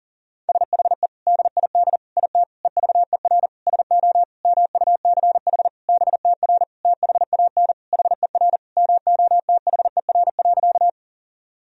Morse code, heard as a keyed tone, about 40 wpm, 700 Hz. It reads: SHE DID IT EVER SO MUCH BTR THAN HER MOTHER.
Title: Sentences from Top 400 Words and Abbreviations 40wpm